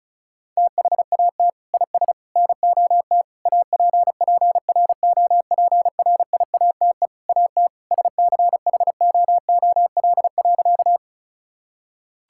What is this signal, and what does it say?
Morse code, heard as a keyed tone, about 34 wpm, 700 Hz. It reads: THAT IS NOT APPROPRIATE AT SCHOOL.
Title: Sentences from Top 1000 Words - Review Part 5 - 35wpm